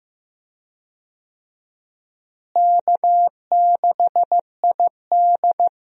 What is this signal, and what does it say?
Morse code, heard as a keyed tone, about 15 wpm, 700 Hz. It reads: K6ID
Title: USA Call Signs - 4 Characters - Repeated 2x - 15wpm